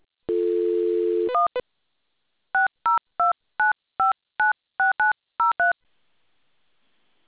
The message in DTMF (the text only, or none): 1 6029596903